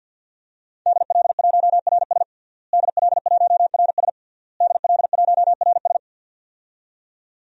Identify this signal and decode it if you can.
Morse code, heard as a keyed tone, about 51 wpm, 700 Hz. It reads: DL1RS DL1RS DL1RS
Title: DXCC-100 Call Signs - 5 Characters - Repeated 3x - 50wpm